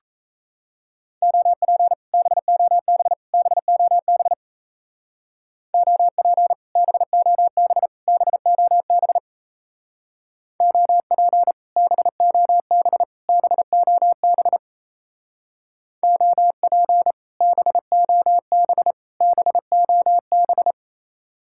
Morse code, heard as a keyed tone, about 36 wpm, 700 Hz. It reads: OP BOB BOB OP BOB BOB OP BOB BOB OP BOB BOB